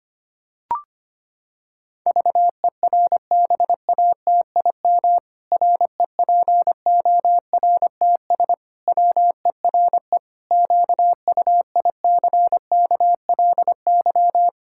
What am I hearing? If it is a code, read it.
Morse code, heard as a keyed tone, about 25 wpm, 700 Hz. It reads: VERBATIM REPORTS WERE QUICKLY
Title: Pangram Challenge v2 - 25wpm